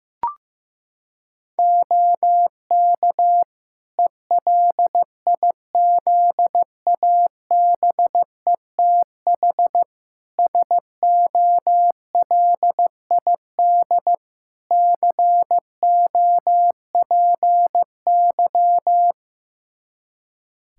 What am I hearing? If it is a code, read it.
Morse code, heard as a keyed tone, about 15 wpm, 700 Hz. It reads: OK ELIZABETH SOLID COPY
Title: Practice QSO - Exchange 3 - More Info - 15wpm